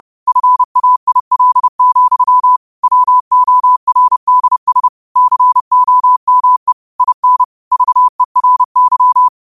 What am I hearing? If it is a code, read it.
Morse code, heard as a keyed tone, about 30 wpm, 1000 Hz. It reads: FAIL, WORDS COME IN VERY